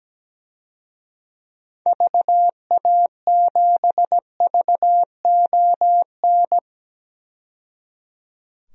Morse code, heard as a keyed tone, about 17 wpm, 700 Hz. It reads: VA7VON
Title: DXCC-100 Call Signs - 6 Characters - 17wpm